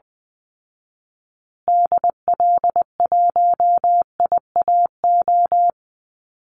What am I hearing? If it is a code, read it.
Morse code, heard as a keyed tone, about 20 wpm, 700 Hz. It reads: DL1IAO
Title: Call Signs Repeated 2x - 20wpm